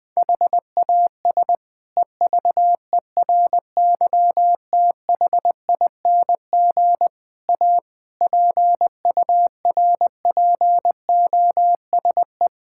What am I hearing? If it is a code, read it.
Morse code, heard as a keyed tone, about 21 wpm, 700 Hz. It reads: HAS EVERYTHING A PURPOSE
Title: Sentences from Top 1600 Words 20wpm